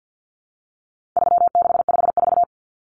Morse code, heard as a keyed tone, about 50 wpm, 700 Hz. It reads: 3654
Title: Common Cut Numbers - 4-Digit 50wpm